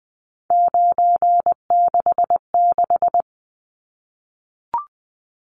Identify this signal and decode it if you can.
Morse code, heard as a keyed tone, about 20 wpm, 700 Hz. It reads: 966
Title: Numbers - 3-Digit 20wpm